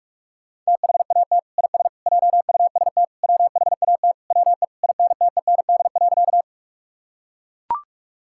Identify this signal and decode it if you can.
Morse code, heard as a keyed tone, about 41 wpm, 700 Hz. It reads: THAT IS JUST WHAT WE INTEND.
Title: Sentences from Top 1600 Words 45wpm